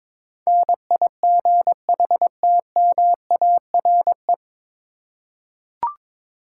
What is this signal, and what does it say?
Morse code, heard as a keyed tone, about 22 wpm, 700 Hz. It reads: NIGHTMARE
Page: Morse Code Ninja practice 22wpm